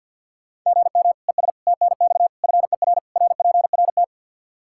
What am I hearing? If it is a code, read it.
Morse code, heard as a keyed tone, about 51 wpm, 700 Hz. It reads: GM ES TNX FER RPRT
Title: Practice QSO - Exchange 1 - Beginning - 50wpm